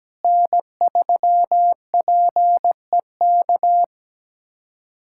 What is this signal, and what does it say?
Morse code, heard as a keyed tone, about 17 wpm, 700 Hz. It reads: N3PEK
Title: USA Callsign DE Callsign 17wpm